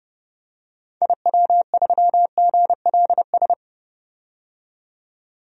Morse code, heard as a keyed tone, about 32 wpm, 700 Hz. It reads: IW3GLS
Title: DXCC-100 Call Signs - 6 Characters - 30wpm